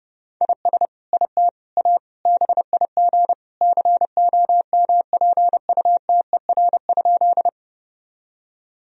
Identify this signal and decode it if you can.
Morse code, heard as a keyed tone, about 30 wpm, 700 Hz. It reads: IS IT A BIG COMPUTER?
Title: Sentences from Top 1000 Words - Review Part 3 - 30wpm